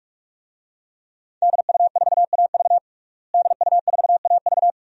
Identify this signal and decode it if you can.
Morse code, heard as a keyed tone, about 46 wpm, 700 Hz. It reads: DU4AV DU4AV
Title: DXCC-100 Call Signs - 5 Characters - Repeated 2x - 45wpm